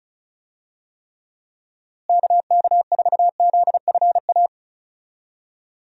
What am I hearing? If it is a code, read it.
Morse code, heard as a keyed tone, about 34 wpm, 700 Hz. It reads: KK4ZFA
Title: USA Call Signs - 6 Characters - Repeated 3x - 35wpm